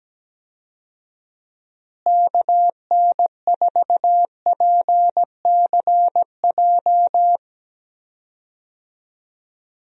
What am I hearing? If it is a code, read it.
Morse code, heard as a keyed tone, about 17 wpm, 700 Hz. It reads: KN4PCJ
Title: USA Call Signs - 6 Characters - Repeated 3x - 17wpm